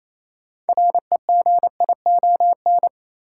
Morse code, heard as a keyed tone, about 28 wpm, 700 Hz. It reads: REGION
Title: Sentences from Top 1500 Words - New Word Drill - 28wpm